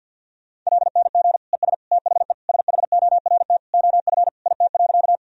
Morse code, heard as a keyed tone, about 48 wpm, 700 Hz. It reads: LNG ES THE SHORT OF IT.
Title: Sentences from Top 400 Words and Abbreviations 50wpm